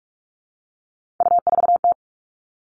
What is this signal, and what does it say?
Morse code, heard as a keyed tone, about 44 wpm, 700 Hz. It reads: U4T
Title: All Cut Numbers - 3-Digit 45wpm